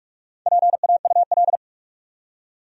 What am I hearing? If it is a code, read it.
Morse code, heard as a keyed tone, about 47 wpm, 700 Hz. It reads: PAUL
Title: USA Common Names - Max 5 Letters - 45wpm